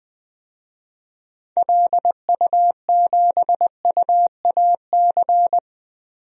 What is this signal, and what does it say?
Morse code, heard as a keyed tone, about 21 wpm, 700 Hz. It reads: LU7UAC
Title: DXCC-100 Call Signs - 6 Characters - Repeated 3x - 20wpm